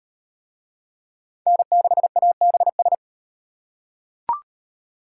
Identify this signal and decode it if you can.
Morse code, heard as a keyed tone, about 35 wpm, 700 Hz. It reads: N6ABS